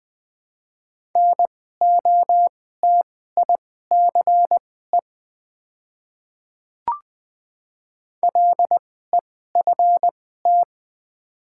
Morse code, heard as a keyed tone, about 20 wpm, spaced as for 14 wpm, 700 Hz. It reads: NOTICE LEFT
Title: 2x Character Spacing - Top 500 Words - 20wpm